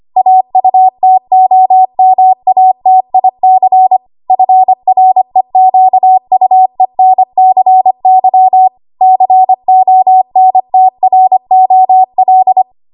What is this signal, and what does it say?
Morse code, heard as a keyed tone, about 25 wpm, 750 Hz. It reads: AUTOMATIC FREQUENCY CONTROL
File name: Archive / 25 WPM / 25WPM